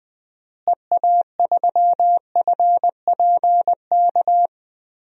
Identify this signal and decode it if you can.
Morse code, heard as a keyed tone, about 21 wpm, 700 Hz. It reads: EA3FPK
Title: DXCC-100 Call Signs - 6 Characters - Repeated 2x - 20wpm